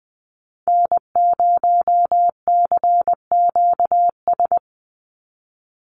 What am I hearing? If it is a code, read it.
Morse code, heard as a keyed tone, about 20 wpm, 700 Hz. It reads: N0CQS
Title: USA Call Signs 20wpm